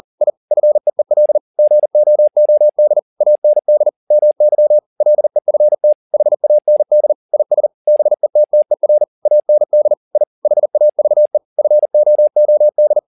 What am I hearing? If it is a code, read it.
Morse code, heard as a keyed tone, about 40 wpm, 600 Hz. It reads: I FEEL GOOD AND MY LEFT HAND IS BETTER AND I HAVE FOOD
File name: the old man and the sea 2 40wpm